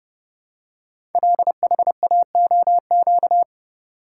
Morse code, heard as a keyed tone, about 30 wpm, 700 Hz. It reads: LHAOQ
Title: Code Groups - 5 Letters - 30wpm